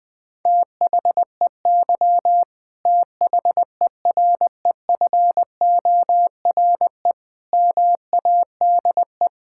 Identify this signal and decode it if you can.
Morse code, heard as a keyed tone, about 20 wpm, 700 Hz. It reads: THEY THEREFORE MADE
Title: Sentences from Top 1000 Words - Review Part 5 - 20wpm